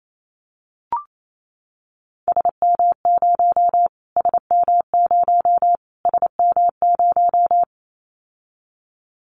Morse code, heard as a keyed tone, about 28 wpm, 700 Hz. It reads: SM0 SM0 SM0
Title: Callsign Prefixes for Top 25 DXCC - Repeated 3x - 28wpm